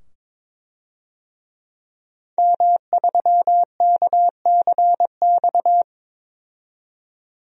Morse code, heard as a keyed tone, about 22 wpm, 700 Hz. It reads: M3KCX
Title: DXCC-100 Call Signs - 5 Characters - 22wpm